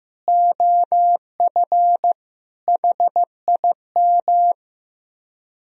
Morse code, heard as a keyed tone, about 15 wpm, 700 Hz. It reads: OF HIM
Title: Speed-Racing - Sets of 4 Words 15wpm